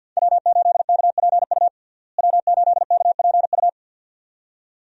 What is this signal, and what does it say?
Morse code, heard as a keyed tone, about 52 wpm, 700 Hz. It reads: W8KPU W8KPU
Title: USA Call Signs - 5 Characters - Repeated 2x - 50wpm